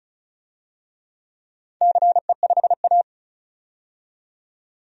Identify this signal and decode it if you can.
Morse code, heard as a keyed tone, about 36 wpm, 690 Hz. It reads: CE5A
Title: DXCC-100 Call Signs - 4 Characters - 35wpm